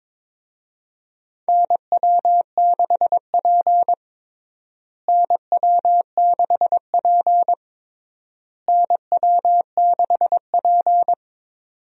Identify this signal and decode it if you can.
Morse code, heard as a keyed tone, about 22 wpm, 700 Hz. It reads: NW6P NW6P NW6P
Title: USA Call Signs - 4 Characters - Repeated 3x - 22wpm